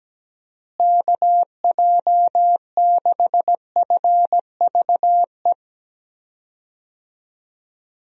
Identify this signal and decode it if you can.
Morse code, heard as a keyed tone, about 17 wpm, 700 Hz. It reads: KJ6FVE
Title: USA Call Signs - 6 Characters - Repeated 3x - 17wpm